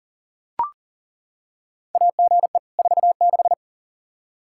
Morse code, heard as a keyed tone, about 40 wpm, 700 Hz. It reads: AGE 46
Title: Practice QSO - Exchange 3 - More Info - 40wpm